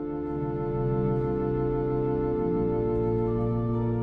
Dial tone (North America), 350 + 440 Hz.